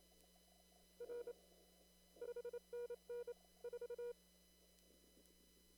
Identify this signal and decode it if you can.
Morse code, heard as a keyed tone, about 27 wpm, 480 Hz. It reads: R 5NN 4
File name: CW / kt5j wpx cw-op-k5pi